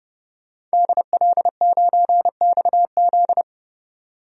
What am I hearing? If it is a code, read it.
Morse code, heard as a keyed tone, about 30 wpm, 700 Hz. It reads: DL9XZ
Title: DXCC-100 Call Signs - 5 Characters - 30wpm